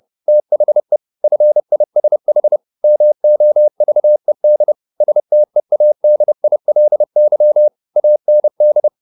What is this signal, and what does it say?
Morse code, heard as a keyed tone, about 30 wpm, 600 Hz. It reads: THE FISH MOVED STEADILY AND
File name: the old man and the sea / the old man and the sea 1 30wpm